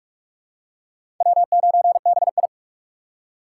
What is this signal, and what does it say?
Morse code, heard as a keyed tone, about 43 wpm, 700 Hz. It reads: W9BI
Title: USA Call Signs - 4 Characters - 3x Character Spacing - 45wpm